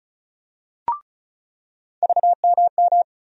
Morse code, heard as a keyed tone, about 34 wpm, 700 Hz. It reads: VMM
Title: USA Call Sign Postfixes - 3 Letters 35wpm